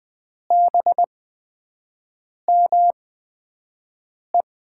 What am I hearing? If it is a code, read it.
Morse code, heard as a keyed tone, about 20 wpm, spaced as for 5 wpm, 700 Hz. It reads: BME
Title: USA Call Signs - 5 Characters - 8x Character Spacing - 20wpm